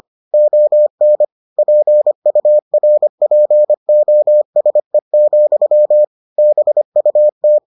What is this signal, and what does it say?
Morse code, heard as a keyed tone, about 25 wpm, 600 Hz. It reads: ON PURPOSE, BUT